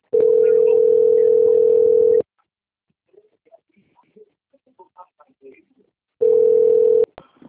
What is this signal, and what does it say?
Ringback tone (North America), 440 + 480 Hz, with speech.
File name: out- - - - - .28524